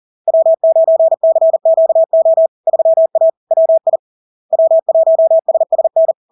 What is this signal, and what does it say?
Morse code, heard as a keyed tone, about 40 wpm, 650 Hz. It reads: W9CQO 3A WI W1SSN
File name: LARK-FD-2022-Log-in-CW-40wpm-650-Hz